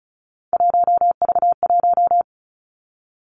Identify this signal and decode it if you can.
Morse code, heard as a keyed tone, about 36 wpm, 700 Hz. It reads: 1V1
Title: Three Challenging Letter-Number Combinations 35wpm